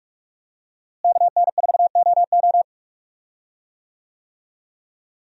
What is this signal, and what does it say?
Morse code, heard as a keyed tone, about 46 wpm, 700 Hz. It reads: KN4OO
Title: USA Call Signs - 5 Characters - Repeated 3x - 45wpm